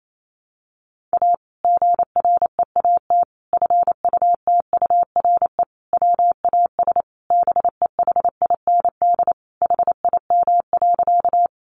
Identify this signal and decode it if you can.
Morse code, heard as a keyed tone, about 28 wpm, 700 Hz. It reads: A GREAT FUTURE WAS BEHIND HIM.
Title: Sentences from Top 700 Words 28wpm